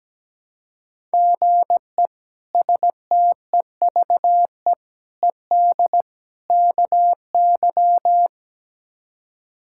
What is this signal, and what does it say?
Morse code, heard as a keyed tone, about 17 wpm, 700 Hz. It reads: GE STEVE ED KY
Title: K1USN SST QSO Practice - 17wpm